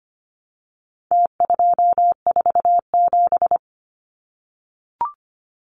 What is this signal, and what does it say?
Morse code, heard as a keyed tone, about 25 wpm, 700 Hz. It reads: T247
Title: Numbers - Contest Style - 4-Digit 25wpm